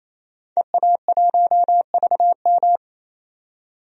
Morse code, heard as a keyed tone, about 28 wpm, 700 Hz. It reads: EA1VM
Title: DXCC-100 Call Signs - 5 Characters - 28wpm